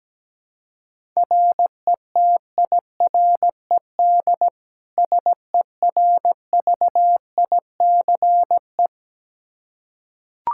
Morse code, heard as a keyed tone, about 17 wpm, 700 Hz. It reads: RETIRED SERVICE
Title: Practice QSO - Exchange 3 - More Info - 17wpm